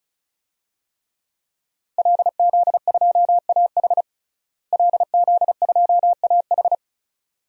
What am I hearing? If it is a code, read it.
Morse code, heard as a keyed tone, about 36 wpm, 700 Hz. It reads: LZ2AH LZ2AH
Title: DXCC-100 Call Signs - 5 Characters - Repeated 2x - 35wpm